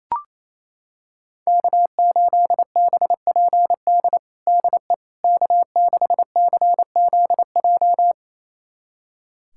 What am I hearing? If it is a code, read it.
Morse code, heard as a keyed tone, about 28 wpm, 700 Hz. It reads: K8BPD DE K6CZJ
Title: USA Callsign DE Callsign - 5 Characters - 28wpm